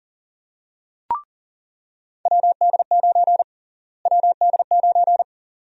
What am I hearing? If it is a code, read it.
Morse code, heard as a keyed tone, about 40 wpm, 700 Hz. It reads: WD9 WD9
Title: USA Call Sign Prefixes - Repeated 2x - 40wpm